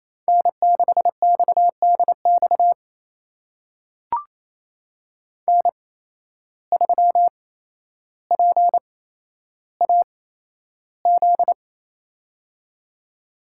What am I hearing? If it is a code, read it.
Morse code, heard as a keyed tone, about 28 wpm, spaced as for 12 wpm, 700 Hz. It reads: N6XDX N3PAZ
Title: USA Call Signs - 5 Characters - 8x Character Spacing - 28wpm